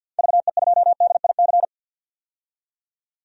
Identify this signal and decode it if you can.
Morse code, heard as a keyed tone, about 52 wpm, 700 Hz. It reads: VE2DIC
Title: DXCC-100 Call Signs - 6 Characters - 50wpm